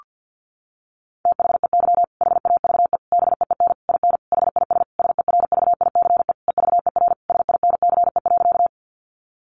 Morse code, heard as a keyed tone, about 50 wpm, 700 Hz. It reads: THEY HAVE BEEN IN HIS SERVICE EVER SINCE.
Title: Sentences from Top 700 Words 50wpm